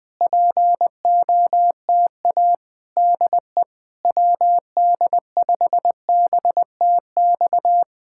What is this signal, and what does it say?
Morse code, heard as a keyed tone, about 20 wpm, 700 Hz. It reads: POTA DE WD5BTX
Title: Speed-Racing Common QSO Phrases Encore - 17wpm